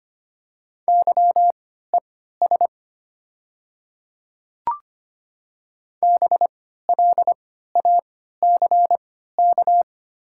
Morse code, heard as a keyed tone, about 25 wpm, spaced as for 14 wpm, 690 Hz. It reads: YES BLACK
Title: 3x Character Spacing - Top 500 Words - 25wpm